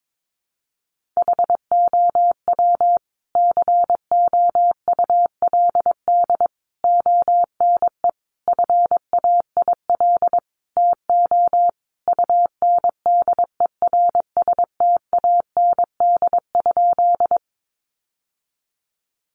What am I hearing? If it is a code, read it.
Morse code, heard as a keyed tone, about 22 wpm, 700 Hz. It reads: HOW COULD ONE FAIL TO UNDERSTAND?
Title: Sentences from Top 1100 Words 22wpm